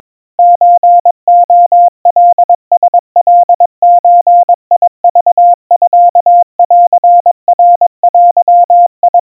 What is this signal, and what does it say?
Morse code (Japanese Wabun code), heard as a keyed tone, about 22 wpm, 700 Hz. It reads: ソレカラカゾクミンナデ
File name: 80 cpm 700 Hz Crispy